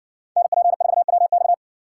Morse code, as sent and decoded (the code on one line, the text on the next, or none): -. .--. ....- .-- -..-
NP4WX